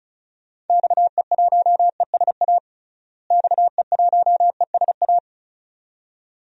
Morse code, heard as a keyed tone, about 36 wpm, 700 Hz. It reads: XE1ESA XE1ESA
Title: DXCC-100 Call Signs - 6 Characters - Repeated 2x - 35wpm